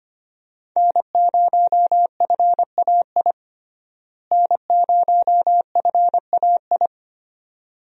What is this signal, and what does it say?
Morse code, heard as a keyed tone, about 25 wpm, 700 Hz. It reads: N0FAI N0FAI
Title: USA Call Signs - 5 Characters - Repeated 3x - 25wpm